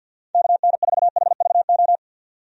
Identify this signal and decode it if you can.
Morse code, heard as a keyed tone, about 50 wpm, 700 Hz. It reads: KN4HVO
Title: USA Call Signs - 6 Characters - 50wpm